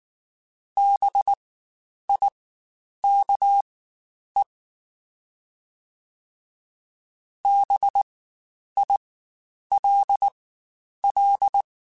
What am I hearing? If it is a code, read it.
Morse code, heard as a keyed tone, about 19 wpm, spaced as for 9 wpm, 800 Hz. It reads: BIKE BILL
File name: Spaced Alphas / Spaced Alphas mp3s / Bs sp @18